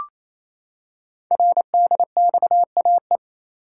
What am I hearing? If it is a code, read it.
Morse code, heard as a keyed tone, about 27 wpm, 700 Hz. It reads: RDXAE
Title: Code Groups - 5 Letters - 28wpm